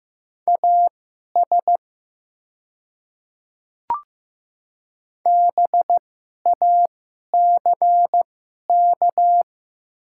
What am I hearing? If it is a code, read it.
Morse code, heard as a keyed tone, about 15 wpm, spaced as for 11 wpm, 700 Hz. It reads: AS BACK
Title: 2x Character Spacing - Top 300 Words - 15wpm